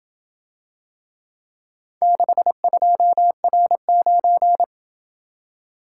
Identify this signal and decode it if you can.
Morse code, heard as a keyed tone, about 27 wpm, 700 Hz. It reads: 62R9